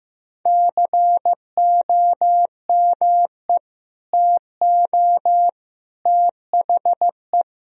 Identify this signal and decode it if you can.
Morse code, heard as a keyed tone, about 15 wpm, 700 Hz. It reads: COME TO THE
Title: Sentences from Top 1000 Words - Review Part 2 - 15wpm